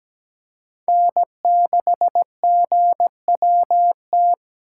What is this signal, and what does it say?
Morse code, heard as a keyed tone, about 17 wpm, 700 Hz. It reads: N6GWT
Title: USA Call Signs - 5 Characters - 4x Character Spacing - 17wpm